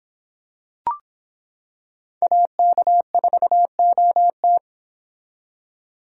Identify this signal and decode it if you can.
Morse code, heard as a keyed tone, about 25 wpm, 700 Hz. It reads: AK4OT